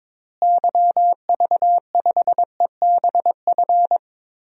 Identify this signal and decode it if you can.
Morse code, heard as a keyed tone, about 22 wpm, 700 Hz. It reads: YV5EBF
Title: DXCC-100 Call Signs - 6 Characters - Repeated 2x - 22wpm